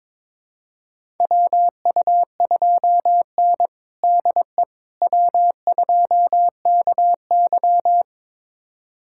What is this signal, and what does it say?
Morse code, heard as a keyed tone, about 21 wpm, 700 Hz. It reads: WU2N DE W2KY
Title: USA Callsign DE Callsign - 4 Characters - 22wpm